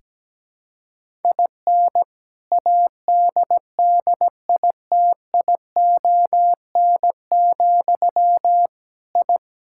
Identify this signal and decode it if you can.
Morse code, heard as a keyed tone, about 17 wpm, 700 Hz. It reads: IN ADDITION, I
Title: Sentences from Top 1000 Words - Review Part 6 - 17wpm